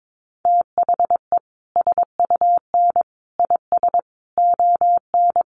Morse code, heard as a keyed tone, about 22 wpm, 700 Hz. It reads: THE SUN IS ON
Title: Sentences from Top 1300 Words 22wpm